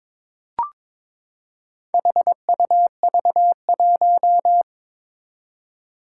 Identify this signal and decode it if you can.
Morse code, heard as a keyed tone, about 22 wpm, 700 Hz. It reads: HUV1
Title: Four Challenging Letter-Number Combinations 22wpm